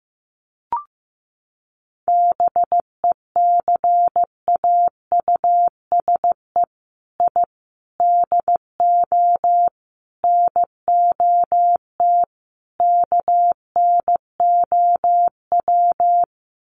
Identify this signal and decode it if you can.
Morse code, heard as a keyed tone, about 15 wpm, 700 Hz. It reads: BECAUSE I DO NOT KNOW
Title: Sets of 5 Words Encore2 15wpm